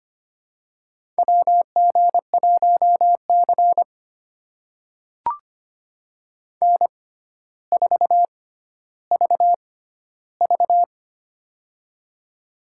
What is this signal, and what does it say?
Morse code, heard as a keyed tone, about 25 wpm, spaced as for 13 wpm, 700 Hz. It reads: WG1C N4VV
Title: USA Call Signs - 4 Characters - 6x Character Spacing - 25wpm